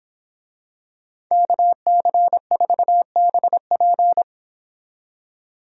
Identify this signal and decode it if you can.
Morse code, heard as a keyed tone, about 26 wpm, 700 Hz. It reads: KC4BP
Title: Speed Racing - USA 2x2 Call Signs 22wpm